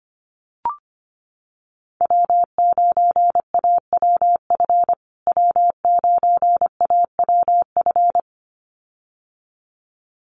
Morse code, heard as a keyed tone, about 25 wpm, 700 Hz. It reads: W9AWF W9AWF
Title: USA Call Signs Repeated 2x 25wpm